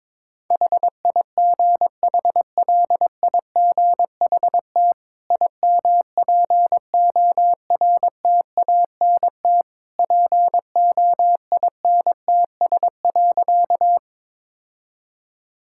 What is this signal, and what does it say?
Morse code, heard as a keyed tone, about 22 wpm, 690 Hz. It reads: HIGHLIGHT IMPORTANT POINTS.
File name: practice / 22wpm / Easy Sentences with Hints - 22wpm